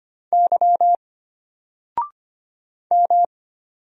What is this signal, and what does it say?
Morse code, heard as a keyed tone, about 25 wpm, 700 Hz. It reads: Y M
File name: Single-Letter-Number-25wpm